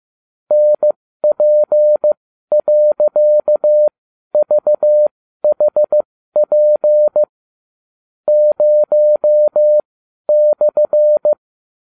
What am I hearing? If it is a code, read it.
Morse code, heard as a keyed tone, about 15 wpm, spaced as for 12 wpm, 600 Hz. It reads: NP.VHP 0/